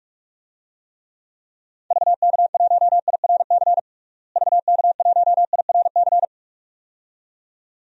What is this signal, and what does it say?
Morse code, heard as a keyed tone, about 46 wpm, 700 Hz. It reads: VK1IRC VK1IRC
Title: DXCC-100 Call Signs - 6 Characters - Repeated 2x - 45wpm